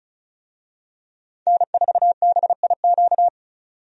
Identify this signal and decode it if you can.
Morse code, heard as a keyed tone, about 36 wpm, 700 Hz. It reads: N4BIQ
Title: USA Call Signs - 5 Characters - 35wpm